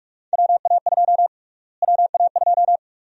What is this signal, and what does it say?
Morse code, heard as a keyed tone, about 46 wpm, 700 Hz. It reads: WA2 WA2
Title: USA Call Sign Prefixes - Repeated 2x - 45wpm